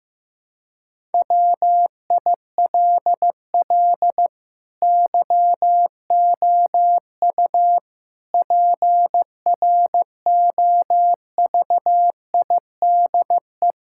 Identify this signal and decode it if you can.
Morse code, heard as a keyed tone, about 15 wpm, 700 Hz. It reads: WILL YOU PROVIDE
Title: Sentences from Top 1600 Words 15wpm